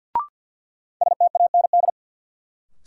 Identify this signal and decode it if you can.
Morse code, heard as a keyed tone, about 53 wpm, 700 Hz. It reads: STAND
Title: Sentences from Top 300 Words - New Word Drill - 50wpm